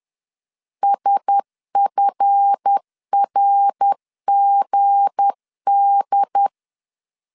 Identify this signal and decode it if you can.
Morse code, heard as a keyed tone, about 10 wpm, 800 Hz. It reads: SFRGD
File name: LEZIONI / 7 GF9Z / 7 GF 10wpm 2514